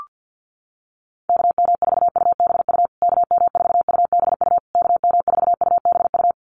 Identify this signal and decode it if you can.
Morse code, heard as a keyed tone, about 50 wpm, 700 Hz. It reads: KM4UBU KM4UBU KM4UBU
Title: USA Call Signs Repeated 3x 50wpm